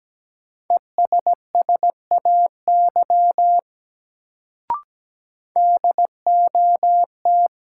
Morse code, heard as a keyed tone, about 17 wpm, 700 Hz. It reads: ESSAY DOT
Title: Sentences from Top 1500 Words - New Word Drill - 17wpm